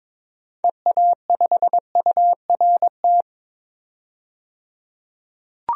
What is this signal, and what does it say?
Morse code, heard as a keyed tone, about 22 wpm, 700 Hz. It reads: EA5URT
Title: DXCC-100 Call Signs - 6 Characters - 22wpm